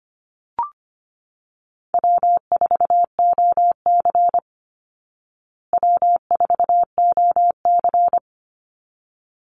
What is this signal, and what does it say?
Morse code, heard as a keyed tone, about 25 wpm, 700 Hz. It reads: W4OC W4OC
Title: Easy Call Signs Repeated 2x - 25wpm